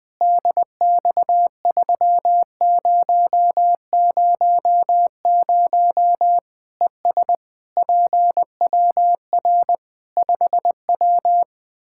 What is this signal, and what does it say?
Morse code, heard as a keyed tone, about 20 wpm, 700 Hz. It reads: DX3000 ES PWR 5W
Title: K1USN SST QSO Practice 20wpm